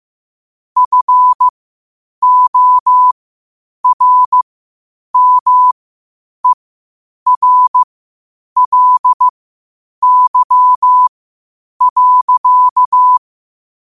Morse code, heard as a keyed tone, about 15 wpm, spaced as for 8 wpm, 1000 Hz. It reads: FORMERLY.